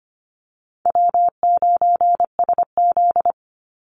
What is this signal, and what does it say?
Morse code, heard as a keyed tone, about 25 wpm, 700 Hz. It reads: W9SZ
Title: Easy Call Signs - 25wpm